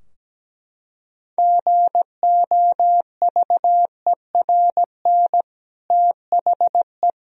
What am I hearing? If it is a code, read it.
Morse code, heard as a keyed tone, about 17 wpm, 700 Hz. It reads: GOVERN THE
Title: Sentences from Top 1000 Words - Review Part 2 - 17wpm